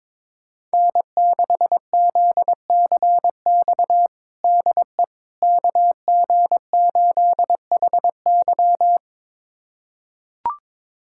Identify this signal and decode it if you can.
Morse code, heard as a keyed tone, about 21 wpm, 700 Hz. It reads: N6ZCX DE KG8HY
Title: USA Callsign DE Callsign 22wpm